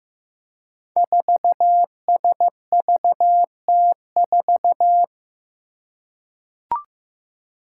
Morse code, heard as a keyed tone, about 15 wpm, 700 Hz. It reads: 4SVT4
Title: Five Challenging Letter-Number Combinations 15wpm